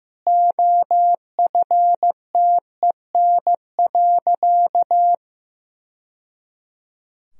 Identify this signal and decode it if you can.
Morse code, heard as a keyed tone, about 16 wpm, 700 Hz. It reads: OFTEN.
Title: Sentences from Top 1000 Words - Review Part 2 - 15wpm